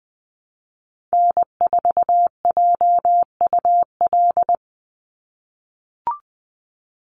Morse code, heard as a keyed tone, about 20 wpm, 700 Hz. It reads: N4JUL
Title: USA Call Signs 20wpm